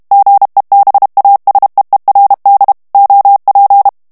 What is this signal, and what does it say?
Morse code, heard as a keyed tone, about 32 wpm, 800 Hz. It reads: GEBASEERD OP
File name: HOLLANDAIS texte clair 1 vitesse 30 mots minute